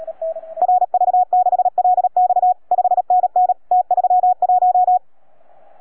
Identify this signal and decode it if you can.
Morse code, heard as a keyed tone, about 37 wpm, 720 Hz. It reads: RV6LX 5NN T31